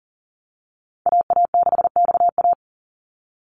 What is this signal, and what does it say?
Morse code, heard as a keyed tone, about 40 wpm, 700 Hz. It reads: AA6XA
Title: NA SOTA Activator Callsigns 40wpm